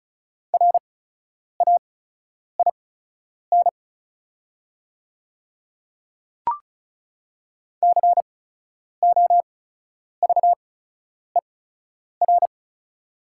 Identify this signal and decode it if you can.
Morse code, heard as a keyed tone, about 36 wpm, spaced as for 10 wpm, 690 Hz. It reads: RAIN COVER
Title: 8x Character Spacing - Top 300 Words - 35wpm